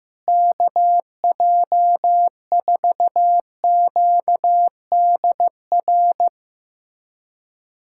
Morse code, heard as a keyed tone, about 15 wpm, 700 Hz. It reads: KJ4QDR
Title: USA Callsign DE Callsign - 6 Characters - 15wpm